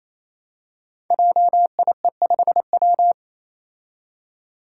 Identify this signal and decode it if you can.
Morse code, heard as a keyed tone, about 29 wpm, 700 Hz. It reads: JIE5W
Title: Code Groups - 5 Letter:Numbers - 28wpm